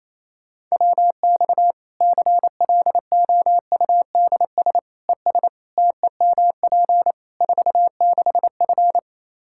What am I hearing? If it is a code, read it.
Morse code, heard as a keyed tone, about 28 wpm, 700 Hz. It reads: WX CLOUDS ES TEMP 46F
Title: Practice QSO - Exchange 2 - Info - 28wpm